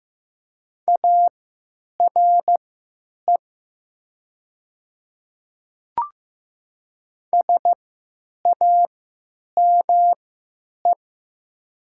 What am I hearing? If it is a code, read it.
Morse code, heard as a keyed tone, about 15 wpm, spaced as for 9 wpm, 700 Hz. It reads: ARE SAME
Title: 3x Character Spacing - Top 300 Words - 15wpm